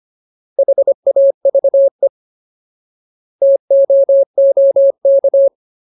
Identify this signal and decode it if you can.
Morse code, heard as a keyed tone, about 25 wpm, 550 Hz. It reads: HAVE TOOK
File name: Ultimate Word List / 25WPM Ultimate